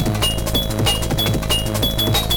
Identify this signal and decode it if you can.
There is no code.